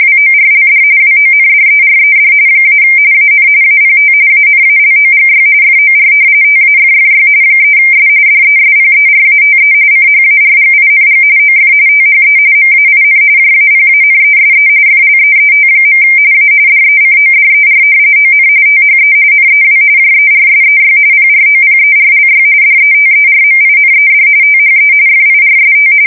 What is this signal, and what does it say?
RTTY, about 45.45 baud. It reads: WORLD WONDERED - AN APT ANALOGY FOR TAFFY 3S DESPERATE CHARGING OF THE MAIN JAPANESE FLEET. A DECODER ON THE NEW JERSEY PROPERLY DETERMINED THE FIRST
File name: fsk170 / battleshipnj